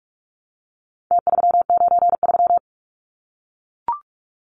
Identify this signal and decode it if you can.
Morse code, heard as a keyed tone, about 44 wpm, 700 Hz. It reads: T393
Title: Numbers - Contest Style - 4-Digit 45wpm